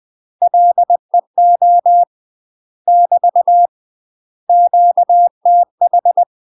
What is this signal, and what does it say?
Morse code, heard as a keyed tone, about 20 wpm, 700 Hz. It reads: LEO <BT> QTH